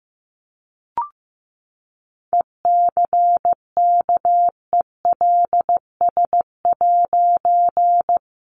Morse code, heard as a keyed tone, about 15 wpm, 700 Hz. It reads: ECKELS'
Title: A Sound Of Thunder 15wpm